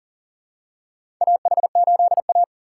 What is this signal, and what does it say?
Morse code, heard as a keyed tone, about 40 wpm, 700 Hz. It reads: AH8A
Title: USA Call Signs - 4 Characters - 4x Character Spacing - 40wpm